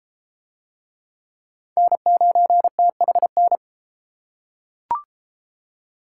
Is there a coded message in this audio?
Yes: Morse code from 1.8 to 3.6 s, heard as a keyed tone, about 31 wpm, 700 Hz. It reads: N9THN